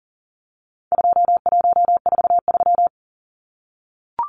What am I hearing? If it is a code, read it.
Morse code, heard as a keyed tone, about 40 wpm, 700 Hz. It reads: 2143
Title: Numbers - 4-Digit 40wpm